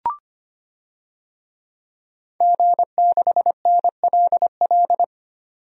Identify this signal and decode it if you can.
Morse code, heard as a keyed tone, about 25 wpm, 700 Hz. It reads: G6NLL
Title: DXCC-100 Call Signs - 5 Characters - 25wpm